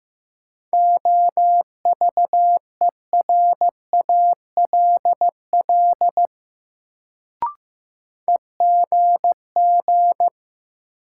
Morse code, heard as a keyed tone, about 15 wpm, 700 Hz. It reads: OVERALL EGG
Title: Sentences from Top 1000 Words - Review Part 7 - New Word Drill - 15wpm